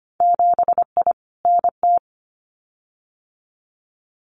Morse code, heard as a keyed tone, about 25 wpm, 700 Hz. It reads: 7I NT
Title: Winter Field Day Exchanges Encore 25wpm